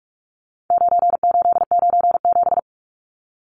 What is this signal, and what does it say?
Morse code, heard as a keyed tone, about 44 wpm, 700 Hz. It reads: 9897
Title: Numbers - 4-Digit 45wpm